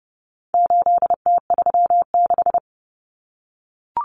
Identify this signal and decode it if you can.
Morse code, heard as a keyed tone, about 30 wpm, 700 Hz. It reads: 8T36
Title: Common Cut Numbers - 4-Digit 30wpm